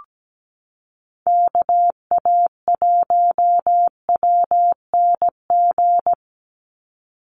Morse code, heard as a keyed tone, about 17 wpm, 700 Hz. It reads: KA1WNG
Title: USA Call Signs 17wpm